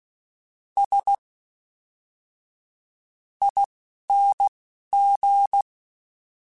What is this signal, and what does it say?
Morse code, heard as a keyed tone, about 16 wpm, spaced as for 7 wpm, 800 Hz. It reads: SING
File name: Special INGs / Special INGs-mp3s / Fast ing @10